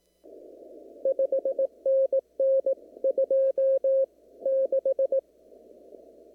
Morse code, heard as a keyed tone, about 18 wpm, spaced as for 16 wpm, 540 Hz. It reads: 5NN 26
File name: CW / kt5j wpx cw-op-k5pi